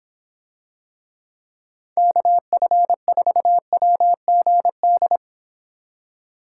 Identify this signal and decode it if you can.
Morse code, heard as a keyed tone, about 25 wpm, 700 Hz. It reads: KF4WGD